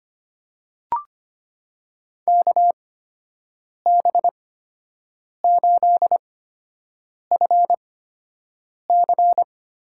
Morse code, heard as a keyed tone, about 25 wpm, spaced as for 7 wpm, 700 Hz. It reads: KB8FC
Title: USA Call Signs - 6 Characters - 8x Character Spacing - 25wpm